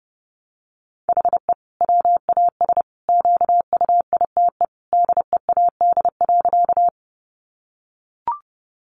Morse code, heard as a keyed tone, about 30 wpm, 700 Hz. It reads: HE WAS QUITE DEAD.
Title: Sentences from Top 1500 Words 30wpm